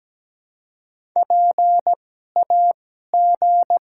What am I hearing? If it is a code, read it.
Morse code, heard as a keyed tone, about 17 wpm, spaced as for 12 wpm, 700 Hz. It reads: PAG
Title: 2x Character Spacing - Top 300 Words - 17wpm